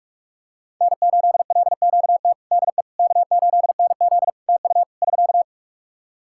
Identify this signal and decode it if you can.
Morse code, heard as a keyed tone, about 46 wpm, 700 Hz. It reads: N8LQT DE K8NZ TU <SK>
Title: Practice QSO - Exchange 4 - Ending - 45wpm